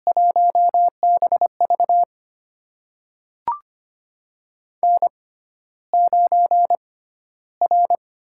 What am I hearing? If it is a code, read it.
Morse code, heard as a keyed tone, about 25 wpm, spaced as for 13 wpm, 700 Hz. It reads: 1BV N9R
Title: USA Call Signs - 5 Characters - 6x Character Spacing - 25wpm